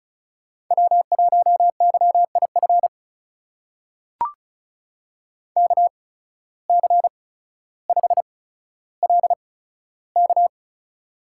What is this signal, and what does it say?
Morse code, heard as a keyed tone, about 36 wpm, spaced as for 15 wpm, 700 Hz. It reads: W1YIF KC5LK
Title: USA Call Signs - 5 Characters - 8x Character Spacing - 35wpm